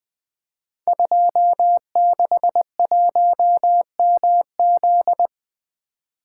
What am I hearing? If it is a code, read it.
Morse code, heard as a keyed tone, about 20 wpm, 700 Hz. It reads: 261MZ
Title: Code Groups - 5 Letter:Numbers - 20wpm